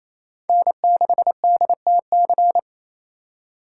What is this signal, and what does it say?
Morse code, heard as a keyed tone, about 28 wpm, 700 Hz. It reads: N6DTC
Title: USA Call Signs - 3x Character Spacing - 28wpm